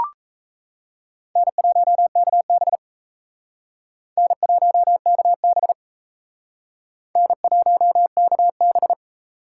Morse code, heard as a keyed tone, about 37 wpm, 700 Hz. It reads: N1KB N1KB N1KB